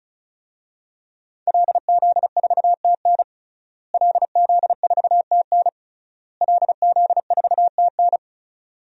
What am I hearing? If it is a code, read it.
Morse code, heard as a keyed tone, about 36 wpm, 700 Hz. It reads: LZ4TN LZ4TN LZ4TN
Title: DXCC-100 Call Signs - 5 Characters - Repeated 3x - 35wpm